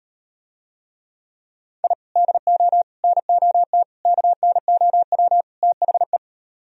Morse code, heard as a keyed tone, about 37 wpm, 700 Hz. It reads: I DO NOT KNOW THE